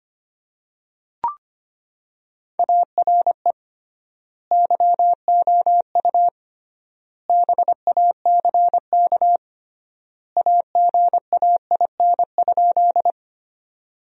Morse code, heard as a keyed tone, about 25 wpm, 700 Hz. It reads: ARE YOU BACK AGAIN?
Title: 3x Word-Spacing - Sentences from Top 200 Words - 25wpm